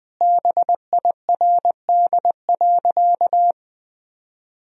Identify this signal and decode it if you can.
Morse code, heard as a keyed tone, about 20 wpm, 700 Hz. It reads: BIRD.
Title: Sentences from Top 1000 Words - Review Part 2 - 20wpm